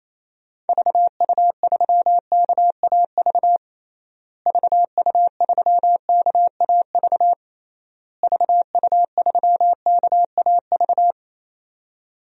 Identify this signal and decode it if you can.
Morse code, heard as a keyed tone, about 28 wpm, 700 Hz. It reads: VU3KAV VU3KAV VU3KAV
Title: DXCC-100 Call Signs - 6 Characters - Repeated 3x - 28wpm